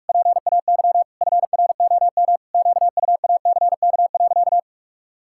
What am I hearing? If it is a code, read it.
Morse code, heard as a keyed tone, about 46 wpm, 700 Hz. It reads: PAY FROM QUACK.
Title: Pangram Challenge v2 - 45wpm